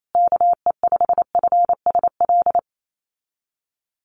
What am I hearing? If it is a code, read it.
Morse code, heard as a keyed tone, about 28 wpm, 700 Hz. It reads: KE5FSL
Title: USA Call Signs Repeated 3x 28wpm